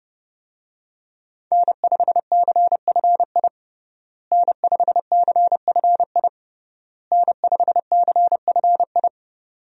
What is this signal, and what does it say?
Morse code, heard as a keyed tone, about 30 wpm, 700 Hz. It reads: N5CFI N5CFI N5CFI
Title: USA Call Signs - 5 Characters - Repeated 3x - 30wpm